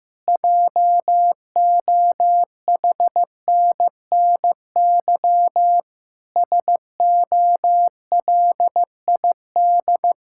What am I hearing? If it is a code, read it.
Morse code, heard as a keyed tone, about 15 wpm, 700 Hz. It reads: JOHNNY SOLID
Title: Practice QSO - Exchange 3 - More Info - 15wpm